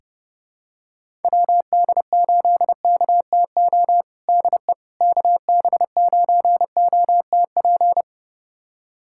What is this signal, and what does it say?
Morse code, heard as a keyed tone, about 32 wpm, 700 Hz. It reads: WD8KTO DE KB9OTP
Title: USA Callsign DE Callsign - 6 Characters - 30wpm